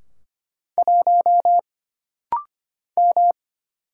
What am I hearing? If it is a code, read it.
Morse code, heard as a keyed tone, about 25 wpm, 700 Hz. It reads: 1 M
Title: Single Letter-Number Alt - Rapid-Fire 25wpm